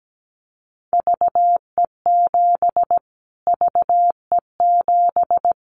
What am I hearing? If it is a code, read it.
Morse code, heard as a keyed tone, about 17 wpm, 700 Hz. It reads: VE7 VE7
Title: Callsign Prefixes for Top 25 DXCC - Repeated 3x - 17wpm